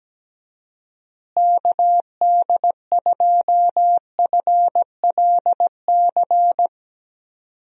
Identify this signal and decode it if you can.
Morse code, heard as a keyed tone, about 17 wpm, 700 Hz. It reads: KD2FLC
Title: USA Call Signs - 6 Characters - Repeated 2x - 17wpm